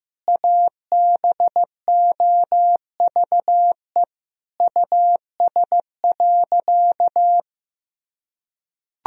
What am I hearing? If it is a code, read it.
Morse code, heard as a keyed tone, about 15 wpm, 700 Hz. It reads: ABOVE US.
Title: Sentences from Top 1000 Words - Review Part 3 - 15wpm